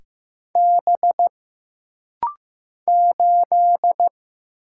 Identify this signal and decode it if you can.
Morse code, heard as a keyed tone, about 15 wpm, 700 Hz. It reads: B 8
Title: Single Letter-Number Alt - Rapid-Fire 15wpm